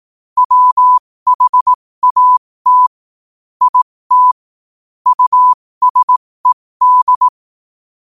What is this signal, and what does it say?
Morse code, heard as a keyed tone, about 18 wpm, spaced as for 15 wpm, 1000 Hz. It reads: WHAT IT USED
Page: New quotes every day in morse code at 15 Words per minute.